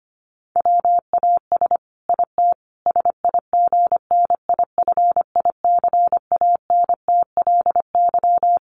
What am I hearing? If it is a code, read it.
Morse code, heard as a keyed tone, about 25 wpm, 700 Hz. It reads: WAS IT SIGNIFICANTLY
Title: Sentences from Top 1500 Words 25wpm